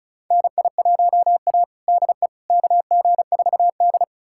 Morse code, heard as a keyed tone, about 36 wpm, 700 Hz. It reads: NI1A DE KG4D
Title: USA Callsign DE Callsign - 4 Characters - 35wpm